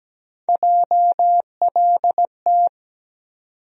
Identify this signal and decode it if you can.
Morse code, heard as a keyed tone, about 17 wpm, 700 Hz. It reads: JLT
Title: USA Call Sign Postfixes - 3 Letters 17wpm